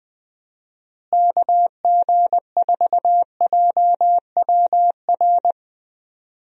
Morse code, heard as a keyed tone, about 20 wpm, 700 Hz. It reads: KG4JWR
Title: USA Call Signs - 6 Characters - Repeated 2x - 20wpm